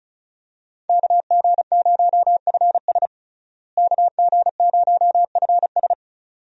Morse code, heard as a keyed tone, about 36 wpm, 700 Hz. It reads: KG0FS KG0FS
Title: USA Call Signs - 5 Characters - Repeated 2x - 35wpm